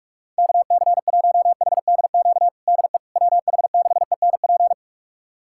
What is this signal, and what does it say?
Morse code, heard as a keyed tone, about 46 wpm, 700 Hz. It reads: KC1HDQ DE WH6ENP
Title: USA Callsign DE Callsign - 6 Characters - 45wpm